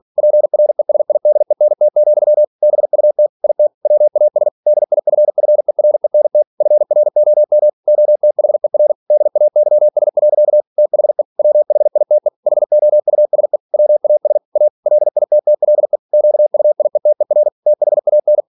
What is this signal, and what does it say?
Morse code, heard as a keyed tone, about 46 wpm, 600 Hz. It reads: PRESIDENT, BUT IT WAS DIFFERENT FROM OTHER DAYS. THE WHITE HOUSE WAS A LITTLE QUIETER THAN
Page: Never 47wpm